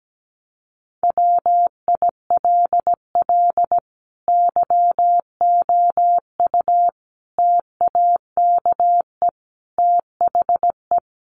Morse code, heard as a keyed tone, about 17 wpm, 700 Hz. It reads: WILL YOU TAKE THE
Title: Sentences from Top 1300 Words 17wpm